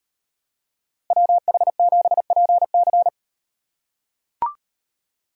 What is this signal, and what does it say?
Morse code, heard as a keyed tone, about 36 wpm, 700 Hz. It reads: WH7PC